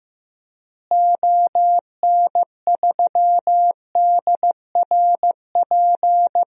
Morse code, heard as a keyed tone, about 15 wpm, 700 Hz. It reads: ON3DRP
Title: DXCC-100 Call Signs - 6 Characters - Repeated 2x - 15wpm